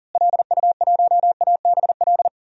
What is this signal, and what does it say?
Morse code, heard as a keyed tone, about 40 wpm, 700 Hz. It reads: LU1ABL
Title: DXCC-100 Call Signs - 6 Characters - Repeated 3x - 40wpm